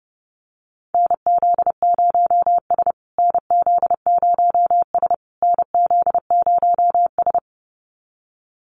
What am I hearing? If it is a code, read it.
Morse code, heard as a keyed tone, about 28 wpm, 700 Hz. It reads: NZ0S NZ0S NZ0S
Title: USA Call Signs Repeated 3x 30wpm